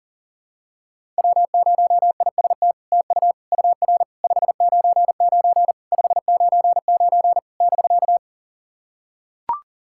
Morse code, heard as a keyed tone, about 40 wpm, 700 Hz. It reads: W0IST TU UR 599 599 <BK>
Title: SOTA QSO Practice 40wpm